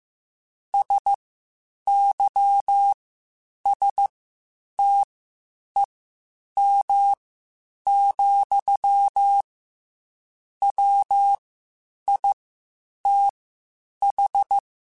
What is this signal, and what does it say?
Morse code, heard as a keyed tone, about 15 wpm, spaced as for 8 wpm, 800 Hz. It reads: SYSTEM, WITH
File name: Quotes Spaced / Quotes Spaced-mp3s / Airtight sp @13